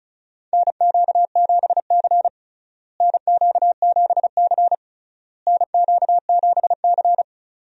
Morse code, heard as a keyed tone, about 34 wpm, 700 Hz. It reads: NQ7C NQ7C NQ7C
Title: USA Call Signs - 4 Characters - Repeated 3x - 35wpm